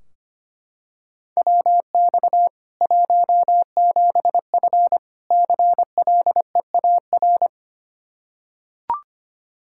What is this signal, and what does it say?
Morse code, heard as a keyed tone, about 25 wpm, 700 Hz. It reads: WX 17F CLEAR
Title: Practice QSO - Exchange 2 - Info - 25wpm